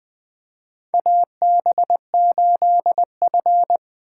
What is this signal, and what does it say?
Morse code, heard as a keyed tone, about 20 wpm, 700 Hz. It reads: AB8F
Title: USA Call Signs - 4 Characters - 8x Character Spacing - 20wpm